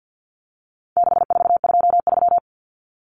Morse code, heard as a keyed tone, about 50 wpm, 700 Hz. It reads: 6423
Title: Numbers - 4-Digit 50wpm